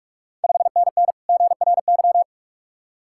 Morse code, heard as a keyed tone, about 41 wpm, 690 Hz. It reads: 5NN GRY
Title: Ontario QSO Party Exchange 45wpm